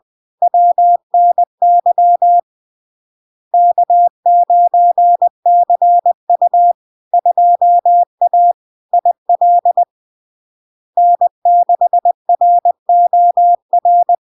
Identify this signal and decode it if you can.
Morse code, heard as a keyed tone, about 20 wpm, 700 Hz. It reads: WNY K9CU 2A IL N6ROR
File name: LARK-FD-2022-Log-in-CW-20wpm-700-Hz